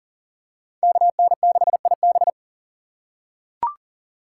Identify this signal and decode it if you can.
Morse code, heard as a keyed tone, about 40 wpm, 700 Hz. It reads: KN6IB
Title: USA Call Signs - 5 Characters - 3x Character Spacing - 40wpm